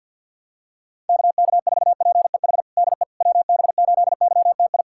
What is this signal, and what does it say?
Morse code, heard as a keyed tone, about 46 wpm, 700 Hz. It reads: KK4PEH DE WB8YTI
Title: USA Callsign DE Callsign - 6 Characters - 50wpm